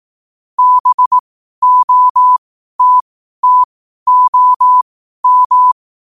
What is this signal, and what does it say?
Morse code, heard as a keyed tone, about 18 wpm, spaced as for 12 wpm, 1000 Hz. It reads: BOTTOM
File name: qotdfiles / 12wpm